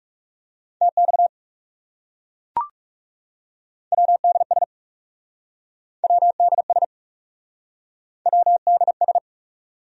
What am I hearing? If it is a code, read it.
Morse code, heard as a keyed tone, about 40 wpm, 700 Hz. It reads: TX WDS WDS WDS